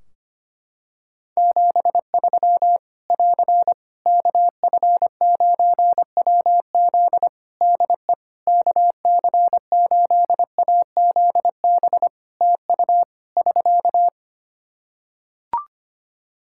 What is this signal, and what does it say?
Morse code, heard as a keyed tone, about 25 wpm, 700 Hz. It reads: 73 <AR> KF9WZ DE KC8AZB TU <SK>
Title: Practice QSO - Exchange 4 - Ending - 25wpm